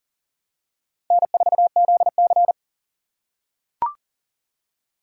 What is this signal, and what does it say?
Morse code, heard as a keyed tone, about 40 wpm, 700 Hz. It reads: N4ZC
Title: USA Call Signs - 4 Characters - 3x Character Spacing - 40wpm